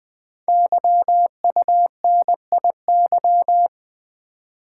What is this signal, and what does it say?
Morse code, heard as a keyed tone, about 20 wpm, 700 Hz. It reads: YUNIY
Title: Code Groups - 5 Letters - 20wpm